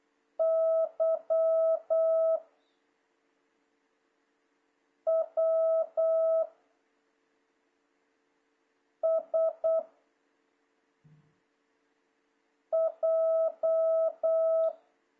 Morse code, heard as a keyed tone, about 8 wpm, 660 Hz. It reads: Y W S J
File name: jh8cbh / 2026cw